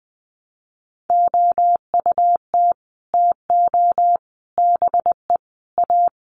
Morse code, heard as a keyed tone, about 20 wpm, 700 Hz. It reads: OUT TO BE A
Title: Sets of 4 Words 20wpm